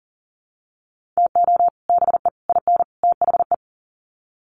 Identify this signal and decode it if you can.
Morse code, heard as a keyed tone, about 38 wpm, 700 Hz. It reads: TO BE IN THE
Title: Sets of 4 Words 40wpm